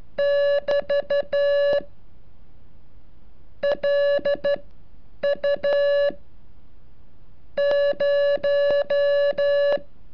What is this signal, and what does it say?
Morse code, heard as a keyed tone, about 13 wpm, spaced as for 8 wpm, 570 Hz. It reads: <BT> LU 0
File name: curso / ejercicio 35